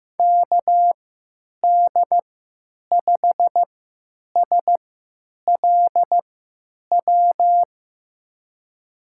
Morse code, heard as a keyed tone, about 15 wpm, spaced as for 9 wpm, 700 Hz. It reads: KD5SLW
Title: USA Call Signs - 6 Characters - 3x Character Spacing - 15wpm